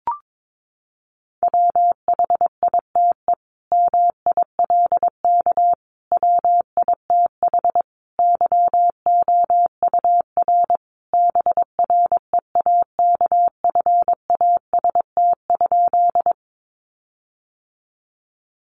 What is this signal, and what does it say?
Morse code, heard as a keyed tone, about 22 wpm, 700 Hz. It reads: WHITE MILK WITH YOUR BREAKFAST?
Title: Sentences from Top 1500 Words 22wpm